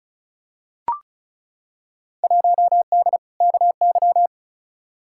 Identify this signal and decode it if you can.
Morse code, heard as a keyed tone, about 36 wpm, 700 Hz. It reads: 1D KY
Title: Field Day Exchanges - 35wpm